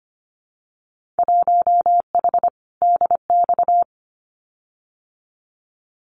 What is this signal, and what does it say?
Morse code, heard as a keyed tone, about 25 wpm, 700 Hz. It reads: 1H DX
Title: Winter Field Day Exchanges 25wpm